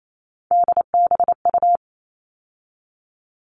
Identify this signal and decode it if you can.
Morse code, heard as a keyed tone, about 28 wpm, 700 Hz. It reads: DBU
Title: All Cut Numbers - 3-Digit 28wpm